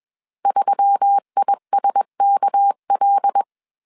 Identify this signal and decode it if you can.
Morse code, heard as a keyed tone, about 21 wpm, 800 Hz. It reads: 3ISKL
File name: CW / DET / 1 6 1500C S B 20wpm 2048